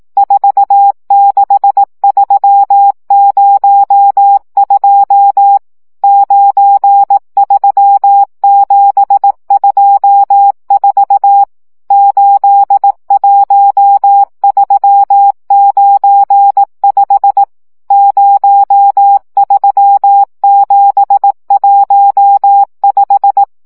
Morse code, heard as a keyed tone, about 18 wpm, 800 Hz. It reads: 46302 93724 81395 03715
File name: morze vizsga